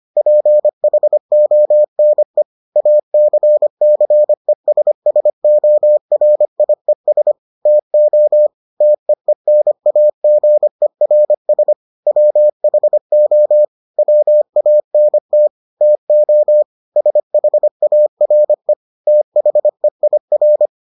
Morse code, heard as a keyed tone, about 25 wpm, 600 Hz. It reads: PHONE ACCESSORIES TO TEENAGERS WHO WANT TO SHARE THEIR